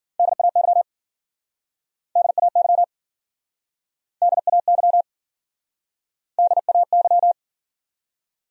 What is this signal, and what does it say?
Morse code, heard as a keyed tone, about 54 wpm, 700 Hz. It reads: DAY DAY DAY DAY